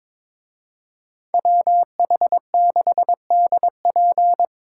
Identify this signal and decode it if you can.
Morse code, heard as a keyed tone, about 25 wpm, 700 Hz. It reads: WH6DP
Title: Speed Racing - USA 2x2 Call Signs 22wpm